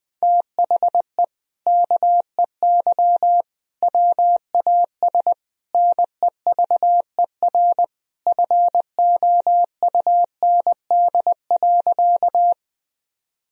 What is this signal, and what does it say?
Morse code, heard as a keyed tone, about 20 wpm, 700 Hz. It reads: THE KEY WAS NEVER FOUND.
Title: Sentences from Top 1000 Words - Review Part 4 - 20wpm